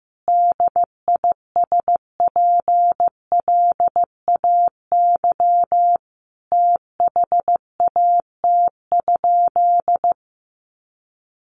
Morse code, heard as a keyed tone, about 15 wpm, 700 Hz. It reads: DISPLAY THAT?
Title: Sentences from Top 1200 Words 15wpm